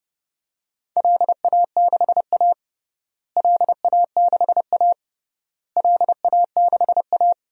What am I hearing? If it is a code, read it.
Morse code, heard as a keyed tone, about 30 wpm, 700 Hz. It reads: LA6A LA6A LA6A
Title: DXCC-100 Call Signs - 4 Characters - Repeated 3x - 30wpm